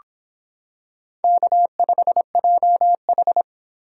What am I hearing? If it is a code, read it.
Morse code, heard as a keyed tone, about 25 wpm, 700 Hz. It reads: K5JH